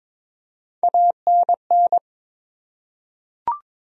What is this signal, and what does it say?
Morse code, heard as a keyed tone, about 22 wpm, 700 Hz. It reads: ANN
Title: USA Common Names - Max 3 Letters - 22wpm